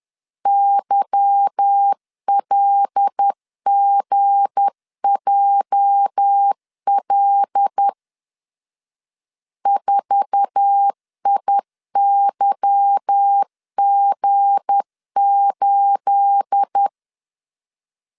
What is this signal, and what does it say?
Morse code, heard as a keyed tone, about 10 wpm, 800 Hz. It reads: YLGJL 4IYG8
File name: CW / DET / 1 9 1500C S B 10wpm 4439